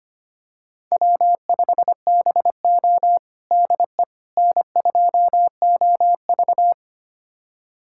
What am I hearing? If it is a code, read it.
Morse code, heard as a keyed tone, about 25 wpm, 700 Hz. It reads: W5BO DE N2OV
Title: USA Callsign DE Callsign - 4 Characters - 25wpm